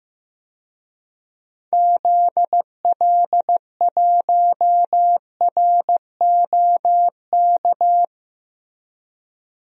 Morse code, heard as a keyed tone, about 15 wpm, 700 Hz. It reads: ZL1ROK
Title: DXCC-100 Call Signs - 6 Characters - Repeated 3x - 15wpm